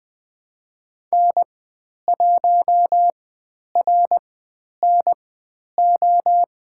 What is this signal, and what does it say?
Morse code, heard as a keyed tone, about 20 wpm, spaced as for 10 wpm, 700 Hz. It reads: N1RNO
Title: 102 Lesson 17g - Letter u - Call Signs 20wpm